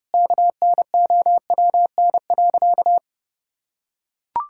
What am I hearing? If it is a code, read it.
Morse code, heard as a keyed tone, about 30 wpm, 700 Hz. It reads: KNOWN.
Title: Sentences from Top 1000 Words - Review Part 6 - 30wpm